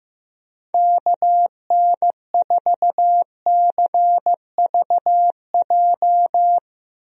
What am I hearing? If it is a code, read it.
Morse code, heard as a keyed tone, about 15 wpm, 700 Hz. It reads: KN4CVJ
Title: USA Call Signs - 6 Characters - Repeated 2x - 15wpm